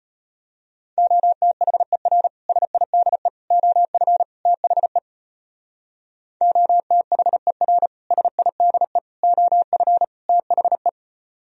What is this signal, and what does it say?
Morse code, heard as a keyed tone, about 33 wpm, 700 Hz. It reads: OTHER SIDE OF THE OTHER SIDE OF THE